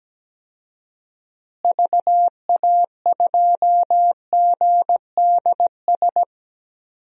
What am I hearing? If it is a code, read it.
Morse code, heard as a keyed tone, about 17 wpm, 700 Hz. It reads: VA2GDS
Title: DXCC-100 Call Signs - 6 Characters - Repeated 2x - 17wpm